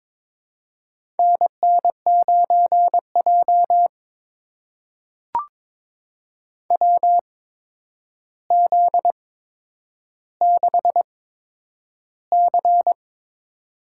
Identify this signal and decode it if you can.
Morse code, heard as a keyed tone, about 22 wpm, spaced as for 9 wpm, 700 Hz. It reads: NN9J WZ6C
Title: USA Call Signs - 4 Characters - 8x Character Spacing - 22wpm